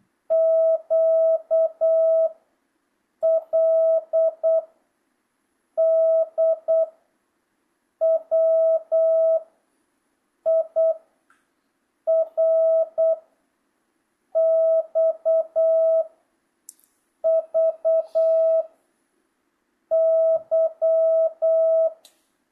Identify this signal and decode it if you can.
Morse code, heard as a keyed tone, about 8 wpm, 650 Hz. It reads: Q L D W I R X V Y
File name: jh8cbh / 2025cw